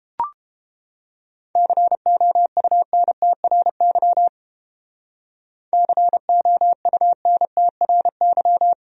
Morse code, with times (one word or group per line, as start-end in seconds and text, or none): 1.6-4.3 COUNTRY
5.7-8.7 COUNTRY